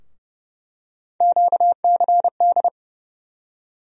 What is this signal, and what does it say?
Morse code, heard as a keyed tone, about 29 wpm, 700 Hz. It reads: QCD
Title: USA Call Sign Postfixes - 3 Letters 30wpm